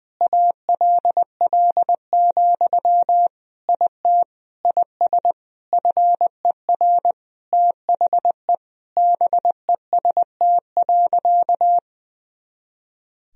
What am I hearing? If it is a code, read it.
Morse code, heard as a keyed tone, about 21 wpm, 700 Hz. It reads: ALL, IT IS FER THE BEST.
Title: Sentences from Top 500 Words and Abbreviations 20wpm